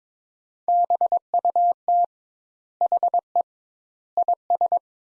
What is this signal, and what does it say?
Morse code, heard as a keyed tone, about 21 wpm, 700 Hz. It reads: BUT HE IS
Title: Sentences Top 200 Words - 22wpm